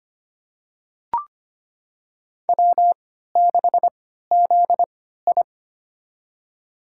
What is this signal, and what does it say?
Morse code, heard as a keyed tone, about 25 wpm, spaced as for 14 wpm, 700 Hz. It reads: W6ZI
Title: USA Call Signs - 4 Characters - 3x Character Spacing - 25wpm